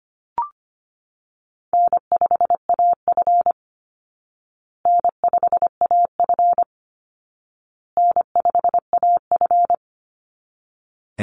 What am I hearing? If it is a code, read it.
Morse code, heard as a keyed tone, about 25 wpm, 700 Hz. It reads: N5AF N5AF N5AF
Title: Easy Call Signs Repeated 3x - 25wpm